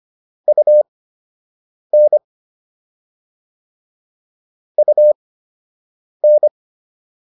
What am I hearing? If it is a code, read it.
Morse code, heard as a keyed tone, about 25 wpm, spaced as for 7 wpm, 600 Hz. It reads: UN UN